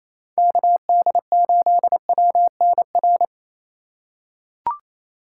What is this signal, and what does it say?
Morse code, heard as a keyed tone, about 27 wpm, 700 Hz. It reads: KD8WNR
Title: USA Call Signs - 6 Characters - 2x Character Spacing - 28wpm